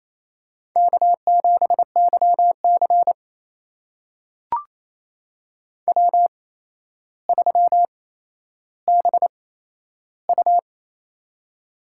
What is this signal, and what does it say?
Morse code, heard as a keyed tone, about 28 wpm, spaced as for 12 wpm, 700 Hz. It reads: K7YC W3BU
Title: USA Call Signs - 4 Characters - 8x Character Spacing - 28wpm